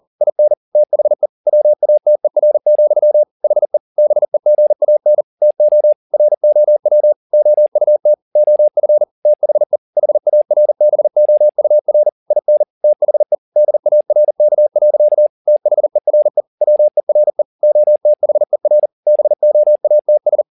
Morse code, heard as a keyed tone, about 40 wpm, 600 Hz. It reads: IN THE WATER, HE BEGAN TO ROW OUT OF THE HARBOUR IN THE DARK. THERE WERE OTHER BOATS